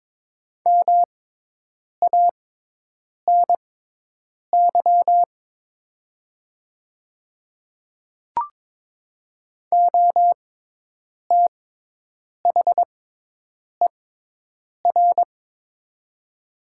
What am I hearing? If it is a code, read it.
Morse code, heard as a keyed tone, about 22 wpm, spaced as for 8 wpm, 700 Hz. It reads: MANY OTHER
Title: 6x Character Spacing - Top 100 Words - 22wpm